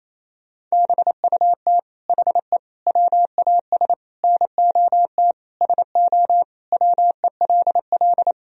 Morse code, heard as a keyed tone, about 28 wpm, 700 Hz. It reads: BUT HE WAS NOT SO WELL
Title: Sentences from Top 1000 Words - Review Part 3 - 28wpm